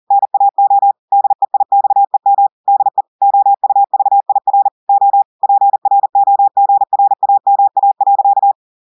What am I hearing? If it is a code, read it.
Morse code, heard as a keyed tone, about 40 wpm, 800 Hz. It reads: NAO DEIXEM DE OUVIR O PROGRAMA.
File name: audios / portugais / portugais texte en clair 2 vitesse 40 mots minutes